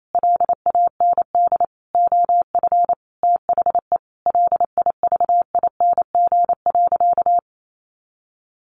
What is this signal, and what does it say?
Morse code, heard as a keyed tone, about 28 wpm, 700 Hz. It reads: LAND OF THE LIVING.
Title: Sentences from Top 1100 Words 28wpm